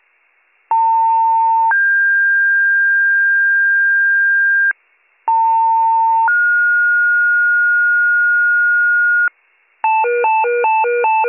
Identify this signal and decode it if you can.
Two-tone page, heard as a one-second tone, then a three-second tone, about 900 then 1600 Hz, about 900 then 1430 Hz, with noise. It is heard from 0.7 to 9.3 s.